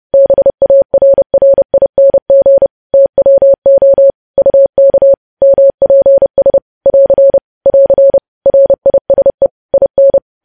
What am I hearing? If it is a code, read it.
Morse code, heard as a keyed tone, about 30 wpm, 550 Hz. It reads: BARRING TWO UK MPS <AR> <AR> RISE IN
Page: news headlines in morse code at 30 wpm